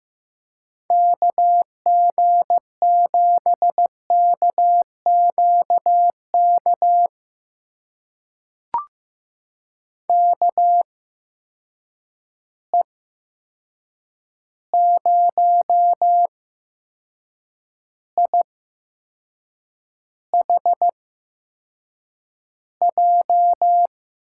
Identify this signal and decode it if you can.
Morse code, heard as a keyed tone, about 15 wpm, spaced as for 6 wpm, 700 Hz. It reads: KG7KQK KE0IHJ
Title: USA Call Signs - 6 Characters - 8x Character Spacing - 15wpm